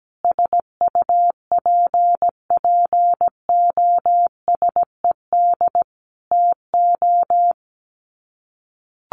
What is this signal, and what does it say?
Morse code, heard as a keyed tone, about 17 wpm, 700 Hz. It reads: SUPPOSED TO
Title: Sets of 5 Words Encore2 17wpm